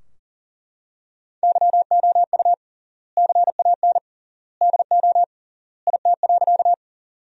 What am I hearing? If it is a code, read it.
Morse code, heard as a keyed tone, about 37 wpm, 700 Hz. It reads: YOU CAN DO IT.
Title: 3x Word-Spacing - Sentences from Top 100 Words - 40wpm